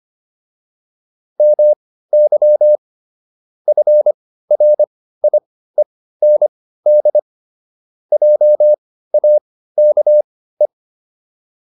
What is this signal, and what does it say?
Morse code, heard as a keyed tone, about 25 wpm, spaced as for 15 wpm, 600 Hz. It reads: MY FRIEND JAKE